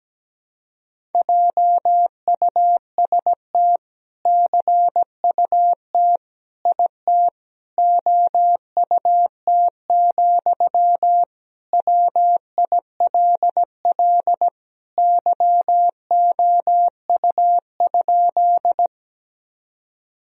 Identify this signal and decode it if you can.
Morse code, heard as a keyed tone, about 17 wpm, 700 Hz. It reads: JUST CUT IT OUT, WILL YOU?
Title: Sentences from Top 1000 Words - Review Part 1 - 17wpm